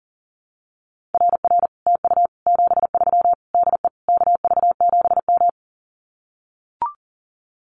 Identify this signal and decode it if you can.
Morse code, heard as a keyed tone, about 40 wpm, 700 Hz. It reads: RR TU 73 DE KV7M
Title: Common QSO Phrases Encore 40wpm